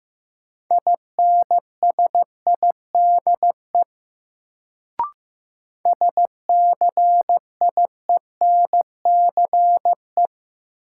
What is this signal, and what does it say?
Morse code, heard as a keyed tone, about 15 wpm, 700 Hz. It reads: INSIDE SCIENCE
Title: Sentences from Top 1000 Words - Review Part 2 - New Word Drill - 15wpm